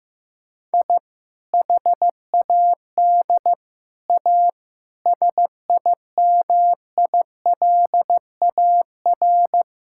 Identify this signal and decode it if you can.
Morse code, heard as a keyed tone, about 15 wpm, 690 Hz. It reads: I HAD A SIMILAR
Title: Sentences from Top 1000 Words - Review Part 5 - 15wpm